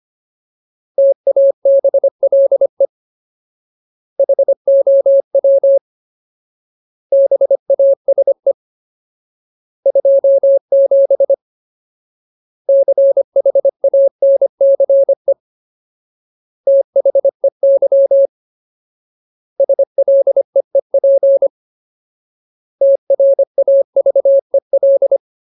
Morse code, heard as a keyed tone, about 25 wpm, 550 Hz. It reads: TABLE HOW BASE 27 CHANCE THEY SLEEP TRAVEL
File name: Ultimate Word List / 25WPM Ultimate